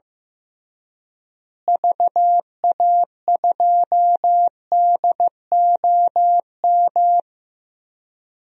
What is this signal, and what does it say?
Morse code, heard as a keyed tone, about 15 wpm, 700 Hz. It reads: VA2DOM
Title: DXCC-100 Call Signs - 6 Characters - Repeated 3x - 15wpm